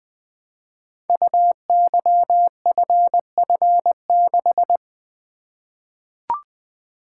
Morse code, heard as a keyed tone, about 20 wpm, 700 Hz. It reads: UYFF6
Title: Code Groups - 5 Letter:Numbers - 20wpm